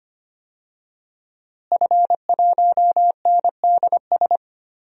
Morse code, heard as a keyed tone, about 25 wpm, 700 Hz. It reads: F1NDS
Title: DXCC-100 Call Signs - 5 Characters - Repeated 3x - 25wpm